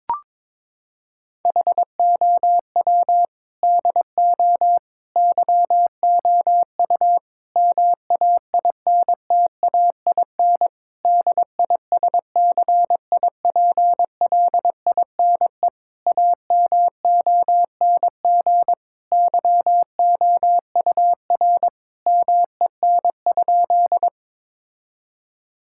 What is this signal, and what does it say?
Morse code, heard as a keyed tone, about 22 wpm, 700 Hz. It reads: HOW DO YOU MAINTAIN DISCIPLINE AMONG YOUR MEN?
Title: Sentences from Top 1000 Words - Review Part 6 - 22wpm